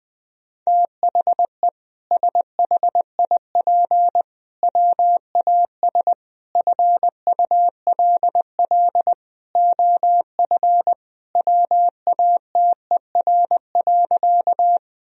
Morse code, heard as a keyed tone, about 20 wpm, 700 Hz. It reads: THE SHIP WAS FULL OF WATER.
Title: Sentences from Top 1000 Words - Review Part 3 - 20wpm